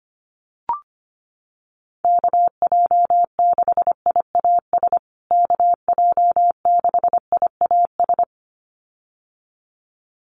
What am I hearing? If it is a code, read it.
Morse code, heard as a keyed tone, about 25 wpm, 700 Hz. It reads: KJ6IAS KJ6IAS
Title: USA Call Signs Repeated 2x 25wpm